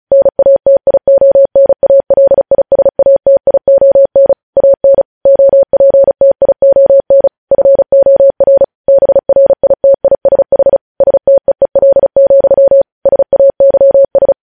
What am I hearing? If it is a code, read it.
Morse code, heard as a keyed tone, about 36 wpm, 550 Hz. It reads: NATIONALISATION AN OPTION FOR BRITISH STEEL, SAYS
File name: podcast / News headlines-1x-35x35